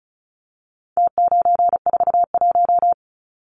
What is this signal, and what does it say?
Morse code, heard as a keyed tone, about 34 wpm, 700 Hz. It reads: T941
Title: Numbers - Contest Style - 4-Digit 35wpm